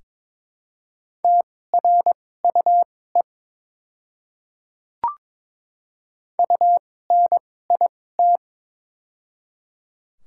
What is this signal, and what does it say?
Morse code, heard as a keyed tone, about 22 wpm, spaced as for 16 wpm, 700 Hz. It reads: TRUE UNIT
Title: 2x Character Spacing - Top 500 Words - 22wpm